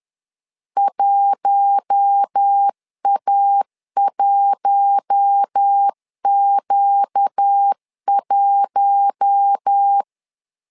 Morse code, heard as a keyed tone, about 10 wpm, 800 Hz. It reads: 1A1Q1
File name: LEZIONI / 4 Q1 10wpm 2448